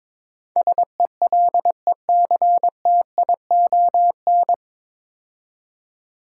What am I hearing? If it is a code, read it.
Morse code, heard as a keyed tone, about 22 wpm, 700 Hz. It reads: SELECTION
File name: 9 Letter Words - First 3 Letters Given - 22wpm